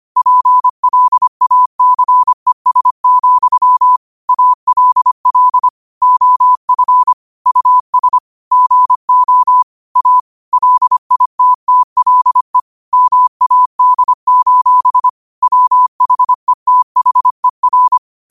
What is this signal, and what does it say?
Morse code, heard as a keyed tone, about 25 wpm, 1000 Hz. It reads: PLACES, ALL OF US GO A LITTLE MAD: WHETHER